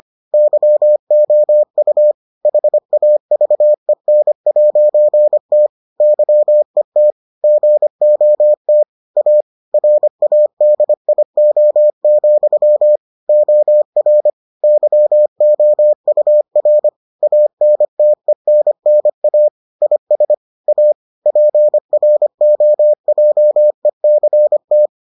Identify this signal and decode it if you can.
Morse code, heard as a keyed tone, about 25 wpm, 600 Hz. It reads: YOU HAVEN'T YET GOT A RADIO, OR YOUR ANTENNA IS A PROJECT